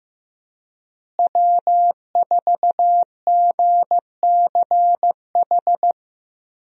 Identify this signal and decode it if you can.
Morse code, heard as a keyed tone, about 15 wpm, 700 Hz. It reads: W4GCH
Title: USA Call Signs - 5 Characters - Repeated 3x - 15wpm